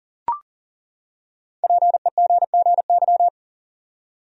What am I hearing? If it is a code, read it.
Morse code, heard as a keyed tone, about 40 wpm, 700 Hz. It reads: PEGGY
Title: USA Common Names - Max 5 Letters - 40wpm